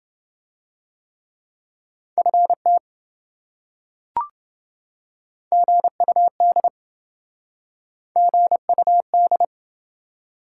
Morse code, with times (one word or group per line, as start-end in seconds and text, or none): 2.2-2.8 FT
5.5-6.7 GUD
8.2-9.4 GUD